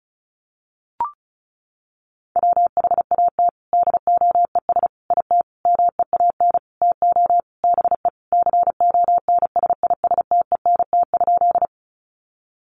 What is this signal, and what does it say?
Morse code, heard as a keyed tone, about 34 wpm, 700 Hz. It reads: WHAT DOES IT MEAN TO BE CONSISTENT?
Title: Sentences from Top 1500 Words 35wpm